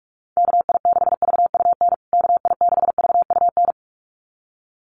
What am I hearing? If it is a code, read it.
Morse code, heard as a keyed tone, about 44 wpm, 700 Hz. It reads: KI6VUN KI6VUN
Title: USA Call Signs Repeated 2x 45wpm